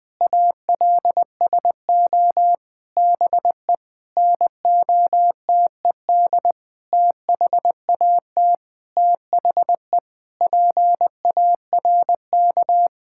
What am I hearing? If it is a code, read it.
Morse code, heard as a keyed tone, about 20 wpm, 700 Hz. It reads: ALSO BE NOTED THAT THE PARK
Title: Sentences with 6-Word Phrases - 20wpm